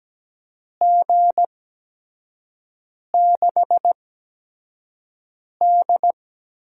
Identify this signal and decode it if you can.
Morse code, heard as a keyed tone, about 17 wpm, spaced as for 5 wpm, 700 Hz. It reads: G6D
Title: USA Call Signs - 4 Characters - 8x Character Spacing - 17wpm